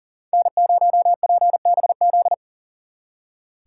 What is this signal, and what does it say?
Morse code, heard as a keyed tone, about 52 wpm, 700 Hz. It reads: N0PBZ